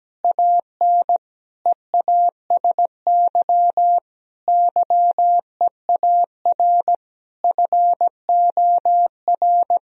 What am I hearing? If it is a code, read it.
Morse code, heard as a keyed tone, about 17 wpm, 700 Hz. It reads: AN EASY YEAR FOR
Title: Sentences from Top 1000 Words - Review Part 2 - 17wpm